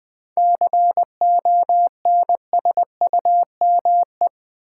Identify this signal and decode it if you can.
Morse code, heard as a keyed tone, about 20 wpm, 700 Hz. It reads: CONSUME
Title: 7 Letter Words - First 3 Letters Given - 20wpm